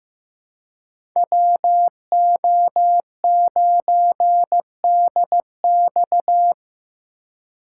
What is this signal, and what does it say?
Morse code, heard as a keyed tone, about 15 wpm, 700 Hz. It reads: WO9DX
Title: USA Call Signs - 5 Characters - 3x Character Spacing - 15wpm